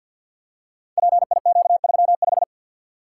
Morse code, heard as a keyed tone, about 53 wpm, 700 Hz. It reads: PIQ35
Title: Code Groups - 5 Letter:Numbers - 50wpm